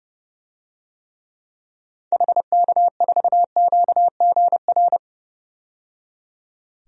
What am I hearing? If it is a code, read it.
Morse code, heard as a keyed tone, about 29 wpm, 690 Hz. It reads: HK4QGR
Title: DXCC-100 Call Signs - 6 Characters - 30wpm